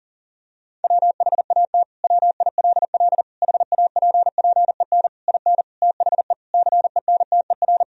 Morse code, heard as a keyed tone, about 40 wpm, 700 Hz. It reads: WHAT WILL HAPPEN IN THE CENTER
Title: Sentences from Top 1000 Words - Review Part 2 - 40wpm